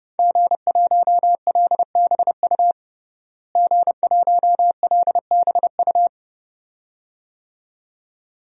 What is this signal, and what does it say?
Morse code, heard as a keyed tone, about 32 wpm, 700 Hz. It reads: G1LBU G1LBU
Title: DXCC-100 Call Signs - 5 Characters - Repeated 2x - 30wpm